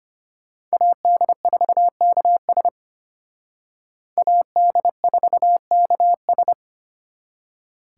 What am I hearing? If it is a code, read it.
Morse code, heard as a keyed tone, about 31 wpm, 700 Hz. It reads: AD4KS AD4KS
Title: Speed Racing - USA 2x2 Call Signs 25wpm